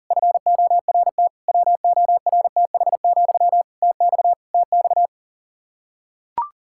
Morse code, heard as a keyed tone, about 40 wpm, 700 Hz. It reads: FORT WORTH, TX TX
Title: Practice QSO - Exchange 1 - Beginning - 40wpm